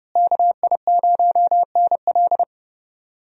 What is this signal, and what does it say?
Morse code, heard as a keyed tone, about 29 wpm, 700 Hz. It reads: KI0NL
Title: USA Call Signs - 5 Characters - Repeated 3x - 30wpm